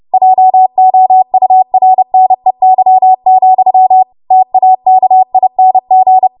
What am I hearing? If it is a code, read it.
Morse code, heard as a keyed tone, about 30 wpm, 750 Hz. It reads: JOURNEY, TAKING
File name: Morse / 30WPM